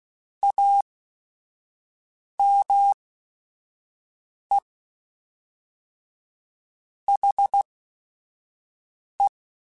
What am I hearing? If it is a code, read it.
Morse code, heard as a keyed tone, about 16 wpm, 800 Hz. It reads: A M E H E